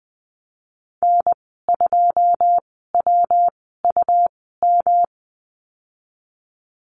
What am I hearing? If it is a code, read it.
Morse code, heard as a keyed tone, about 20 wpm, spaced as for 14 wpm, 700 Hz. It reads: N2WUM
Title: 219 Lesson 35g - Number 0 - Call Signs 20wpm